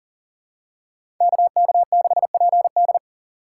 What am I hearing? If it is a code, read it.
Morse code, heard as a keyed tone, about 40 wpm, 700 Hz. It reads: KK6PD
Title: USA Call Signs - 5 Characters - 2x Character Spacing - 40wpm